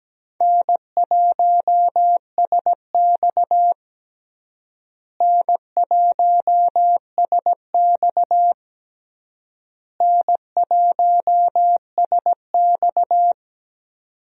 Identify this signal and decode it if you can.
Morse code, heard as a keyed tone, about 17 wpm, 700 Hz. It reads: N1SX N1SX N1SX
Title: USA Call Signs - 4 Characters - Repeated 3x - 17wpm